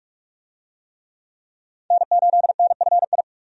Morse code, heard as a keyed tone, about 41 wpm, 700 Hz. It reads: N8NFI
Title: USA Call Signs - 5 Characters - 45wpm